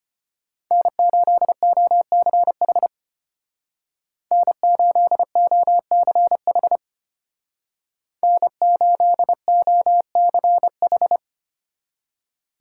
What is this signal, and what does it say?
Morse code, heard as a keyed tone, about 30 wpm, 700 Hz. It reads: N8OCH N8OCH N8OCH
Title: Speed Racing - USA 1x3 Call Signs 25wpm